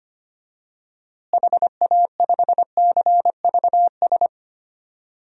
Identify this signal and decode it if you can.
Morse code, heard as a keyed tone, about 25 wpm, 700 Hz. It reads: HA5CVS
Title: DXCC-100 Call Signs - 6 Characters - Repeated 2x - 25wpm